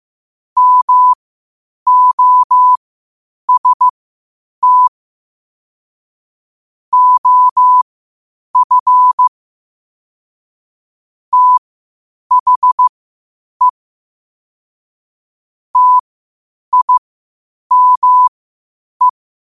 Morse code, heard as a keyed tone, about 15 wpm, spaced as for 8 wpm, 1000 Hz. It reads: MOST OF THE TIME
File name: qotdfiles / 8wpm